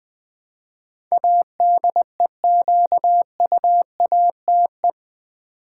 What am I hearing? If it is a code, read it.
Morse code, heard as a keyed tone, about 20 wpm, 700 Hz. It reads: ADEQUATE
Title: Build A Word - 4 Letters At A Time - 10-Letter Words - 20wpm